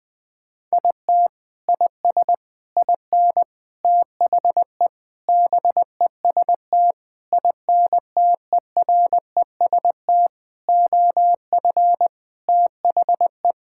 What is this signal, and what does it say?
Morse code, heard as a keyed tone, about 21 wpm, 700 Hz. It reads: IT IS IN THE BEST INTEREST OF THE
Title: Sentences with 6-Word Phrases - 20wpm